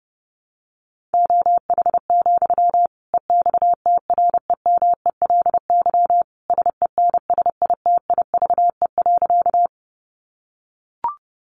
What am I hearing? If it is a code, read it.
Morse code, heard as a keyed tone, about 30 wpm, 700 Hz. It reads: OH, EXTREMELY SENSITIVE.
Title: Sentences from Top 1400 Words 30wpm